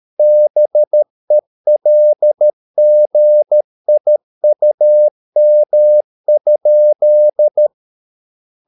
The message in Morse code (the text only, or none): BELGIUM?